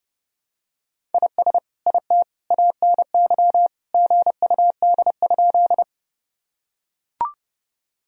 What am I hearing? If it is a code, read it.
Morse code, heard as a keyed tone, about 30 wpm, 700 Hz. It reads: IS IT ANY GUD?
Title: Sentences from Top 200 Words and Abbreviations 30wpm